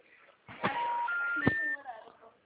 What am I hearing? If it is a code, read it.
Special information tone, with speech.